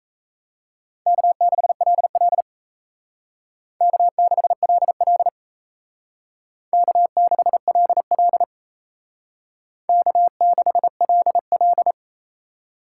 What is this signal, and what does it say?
Morse code, heard as a keyed tone, about 35 wpm, 700 Hz. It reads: K6LL K6LL K6LL K6LL